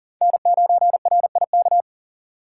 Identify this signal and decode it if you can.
Morse code, heard as a keyed tone, about 40 wpm, 700 Hz. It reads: N9RIK
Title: USA Call Signs - 5 Characters - 2x Character Spacing - 40wpm